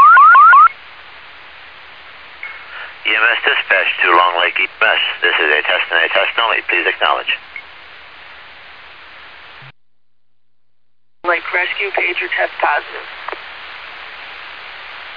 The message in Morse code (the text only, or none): none